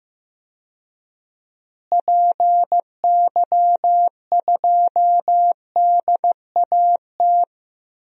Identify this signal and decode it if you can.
Morse code, heard as a keyed tone, about 15 wpm, 700 Hz. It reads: PY2DAT
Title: DXCC-100 Call Signs - 6 Characters - Repeated 3x - 15wpm